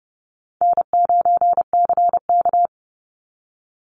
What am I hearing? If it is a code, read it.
Morse code, heard as a keyed tone, about 30 wpm, 700 Hz. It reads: N9CK
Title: Easy Call Signs Repeated 2x - 30wpm